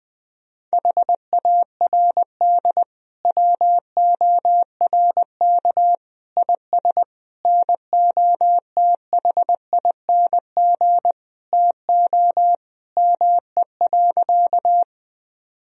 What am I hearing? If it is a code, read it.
Morse code, heard as a keyed tone, about 20 wpm, 700 Hz. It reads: HARD WORK IS NOTHING TO ME.
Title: Sentences from Top 1000 Words - Review Part 1 - 20wpm